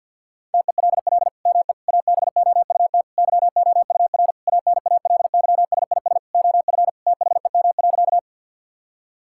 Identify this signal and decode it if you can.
Morse code, heard as a keyed tone, about 50 wpm, 690 Hz. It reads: TELL ME ABOUT YOUR ANALYSIS OF THEM.
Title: Sentences from Top 1000 Words - Review Part 5 - 50wpm